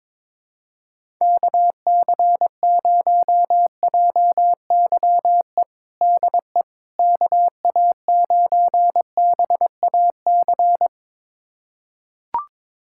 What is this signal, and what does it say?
Morse code, heard as a keyed tone, about 22 wpm, 700 Hz. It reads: KC0JYE DE KA9BAC
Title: USA Callsign DE Callsign - 6 Characters - 22wpm